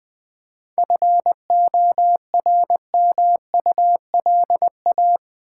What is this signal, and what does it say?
Morse code, heard as a keyed tone, about 20 wpm, 700 Hz. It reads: FORMULA
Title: 7 Letter Words - First 3 Letters Given - 20wpm